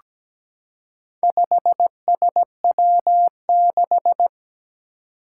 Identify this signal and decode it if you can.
Morse code, heard as a keyed tone, about 17 wpm, 700 Hz. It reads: 5SW6
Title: Four Challenging Letter-Number Combinations 17wpm